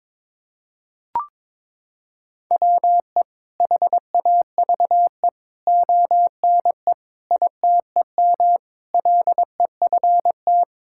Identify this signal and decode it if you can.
Morse code, heard as a keyed tone, about 22 wpm, 700 Hz. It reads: WE HAVE ONE ITEM LEFT
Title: Sentences from Top 1000 Words - Review Part 6 - 22wpm